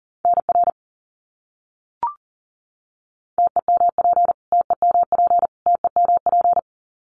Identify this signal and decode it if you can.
Morse code, heard as a keyed tone, about 40 wpm, 700 Hz. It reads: NR TEMP TEMP TEMP
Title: Abbreviations Repeated 3x - 40wpm